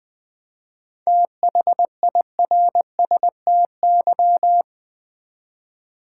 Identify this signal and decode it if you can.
Morse code, heard as a keyed tone, about 20 wpm, 690 Hz. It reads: THIRSTY
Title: 7 Letter Words - First 3 Letters Given - 20wpm